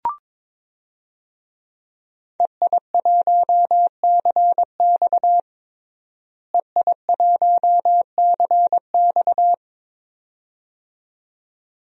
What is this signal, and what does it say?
Morse code, heard as a keyed tone, about 22 wpm, 700 Hz. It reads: EI1CX EI1CX
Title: DXCC-100 Call Signs - 5 Characters - Repeated 2x - 22wpm